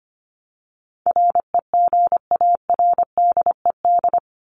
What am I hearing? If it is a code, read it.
Morse code, heard as a keyed tone, about 25 wpm, 700 Hz. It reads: REGARDED
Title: ED Suffix - Top 100 Words with Base Words 25wpm